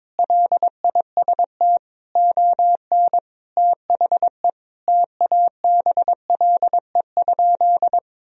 Morse code, heard as a keyed tone, about 22 wpm, 700 Hz. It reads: LIST ON THE TABLE?
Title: Sentences from Top 1000 Words - Review Part 2 - 22wpm